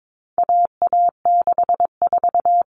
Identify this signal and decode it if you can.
Morse code, heard as a keyed tone, about 22 wpm, 700 Hz. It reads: AA64
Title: All Cut Numbers - 4-Digit 22wpm